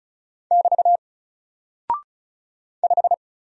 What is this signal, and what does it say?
Morse code, heard as a keyed tone, about 36 wpm, 700 Hz. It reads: <BT> 5
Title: Single Character (Letter:Number:Punctuation:Prosign) Alt 35wpm